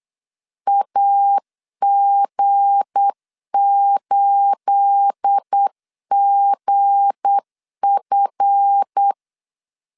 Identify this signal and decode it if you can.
Morse code, heard as a keyed tone, about 8 wpm, 800 Hz. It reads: AG8GF
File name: LEZIONI / 10 SIGNS / 10B INT 8wpm 3534